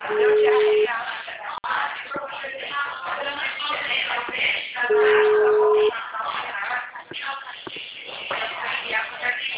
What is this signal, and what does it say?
Ringback tone (Europe, 425 Hz), with speech.
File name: out- -2001- - - .86144